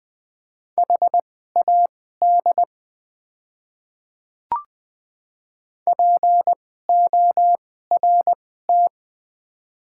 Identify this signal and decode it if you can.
Morse code, heard as a keyed tone, about 20 wpm, spaced as for 15 wpm, 690 Hz. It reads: HAD PORT
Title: 2x Character Spacing - Top 200 Words - 20wpm